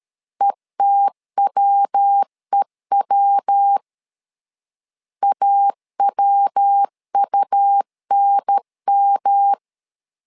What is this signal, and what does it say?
Morse code, heard as a keyed tone, about 13 wpm, 800 Hz. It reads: ETWEW AWUNM
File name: LEZIONI / 3 MWRO / 3 MW 12wpm 1633